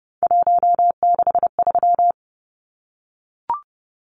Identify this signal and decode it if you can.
Morse code, heard as a keyed tone, about 30 wpm, 700 Hz. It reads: 163
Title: Numbers - 3-Digit 30wpm